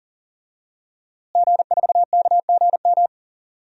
Morse code, heard as a keyed tone, about 40 wpm, 700 Hz. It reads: G4KGM
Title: DXCC-100 Call Signs - 5 Characters - Repeated 2x - 40wpm